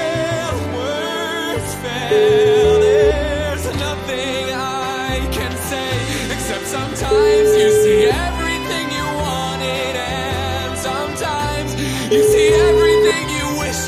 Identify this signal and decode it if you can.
Ringback tone (Europe, 425 Hz), with noise.